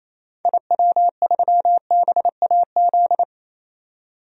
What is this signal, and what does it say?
Morse code, heard as a keyed tone, about 28 wpm, 700 Hz. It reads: IW3BAZ
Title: DXCC-100 Call Signs - 6 Characters - 28wpm